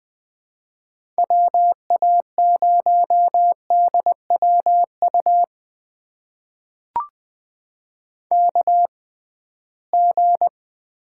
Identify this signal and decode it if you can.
Morse code, heard as a keyed tone, about 20 wpm, spaced as for 15 wpm, 700 Hz. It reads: WA0DWU KG
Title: USA Call Signs - 6 Characters - 6x Character Spacing - 20wpm